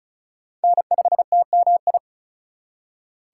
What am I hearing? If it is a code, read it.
Morse code, heard as a keyed tone, about 44 wpm, 700 Hz. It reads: N5TMI
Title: Speed Racing - USA Call Signs 35wpm